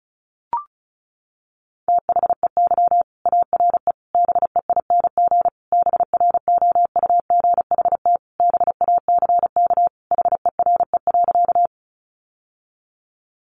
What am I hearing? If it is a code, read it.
Morse code, heard as a keyed tone, about 34 wpm, 700 Hz. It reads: THEY ARE BEING BROUGHT BACK HERE.
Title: Sentences from Top 600 Words 35wpm